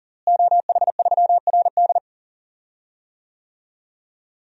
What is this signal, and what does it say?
Morse code, heard as a keyed tone, about 40 wpm, 700 Hz. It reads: OH3RD
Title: DXCC-100 Call Signs - 5 Characters - Repeated 3x - 40wpm